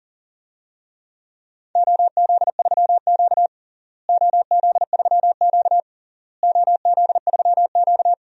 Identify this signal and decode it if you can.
Morse code, heard as a keyed tone, about 40 wpm, 700 Hz. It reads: OZ3Q OZ3Q OZ3Q
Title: DXCC-100 Call Signs - 4 Characters - Repeated 3x - 40wpm